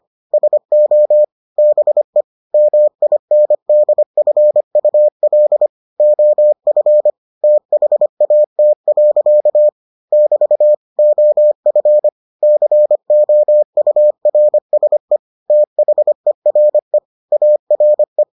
Morse code, heard as a keyed tone, about 25 wpm, 600 Hz. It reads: SO BE MINDFUL OF THAT. <BT> OF COURSE THERE ARE